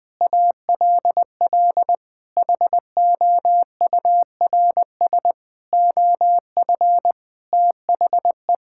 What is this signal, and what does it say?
Morse code, heard as a keyed tone, about 20 wpm, 700 Hz. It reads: ALL HOURS OF THE
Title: Sentences with 8-Word Phrases - 20wpm